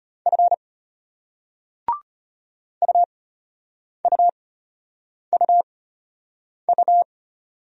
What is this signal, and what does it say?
Morse code, heard as a keyed tone, about 34 wpm, spaced as for 8 wpm, 700 Hz. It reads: F UUUU